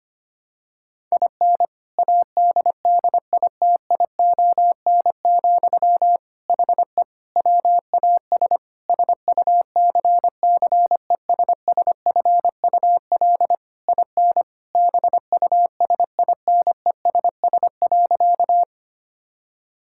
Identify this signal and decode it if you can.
Morse code, heard as a keyed tone, about 25 wpm, 700 Hz. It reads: IN ADDITION, HE WAS SUCCESSFUL IN BUSINESS.
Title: Sentences from Top 1000 Words - Review Part 6 - 25wpm